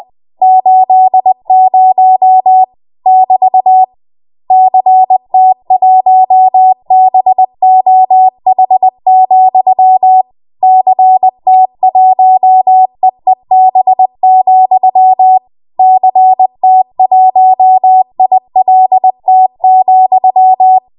Morse code, heard as a keyed tone, about 20 wpm, 750 Hz. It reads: 80 <BT> CT1BOH, CT1EEB, CT1ILT,
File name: Morse / 20WPM